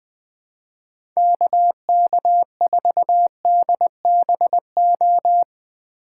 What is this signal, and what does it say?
Morse code, heard as a keyed tone, about 20 wpm, 700 Hz. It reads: KK4DBO
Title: Speed Racing - USA 2x3 Call Signs 17wpm